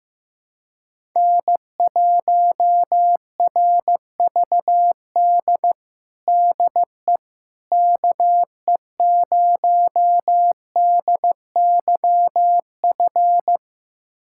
Morse code, heard as a keyed tone, about 16 wpm, 700 Hz. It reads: N1RVD DE KE0DYF
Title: USA Callsign DE Callsign 15wpm